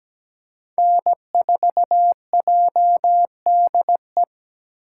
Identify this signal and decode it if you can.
Morse code, heard as a keyed tone, about 17 wpm, 700 Hz. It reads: N4JDE
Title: USA Call Signs - 5 Characters - 6x Character Spacing - 17wpm